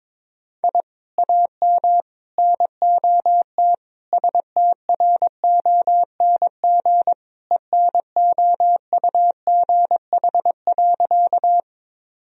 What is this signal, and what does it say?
Morse code, heard as a keyed tone, about 22 wpm, 700 Hz. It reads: I AM NOT STRONG ENOUGH.
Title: Sentences from Top 1000 Words - Review Part 3 - 22wpm